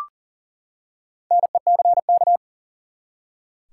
Morse code, heard as a keyed tone, about 40 wpm, 700 Hz. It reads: NECK
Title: Four Letter Words - 40wpm